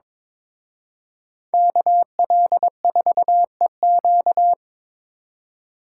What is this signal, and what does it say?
Morse code, heard as a keyed tone, about 26 wpm, 700 Hz. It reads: KL4EQ
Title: Speed Racing - USA 2x2 Call Signs 22wpm